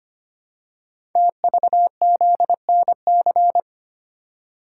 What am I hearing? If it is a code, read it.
Morse code, heard as a keyed tone, about 25 wpm, 700 Hz. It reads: TVZNC
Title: Code Groups - 5 Letter:Numbers - 25wpm